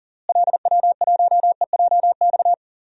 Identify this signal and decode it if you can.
Morse code, heard as a keyed tone, about 37 wpm, 700 Hz. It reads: LW1EJX
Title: DXCC-100 Call Signs - 6 Characters - Repeated 2x - 40wpm